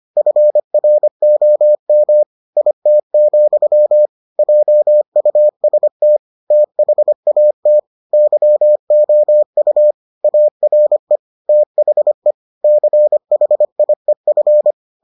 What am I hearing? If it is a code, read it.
Morse code, heard as a keyed tone, about 25 wpm, 600 Hz. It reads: FROM IT, JUST THAT YOU ARE THE CHIEF